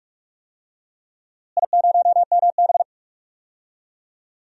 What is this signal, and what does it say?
Morse code, heard as a keyed tone, about 46 wpm, 700 Hz. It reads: I0MB
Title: DXCC-100 Call Signs - 4 Characters - Repeated 3x - 45wpm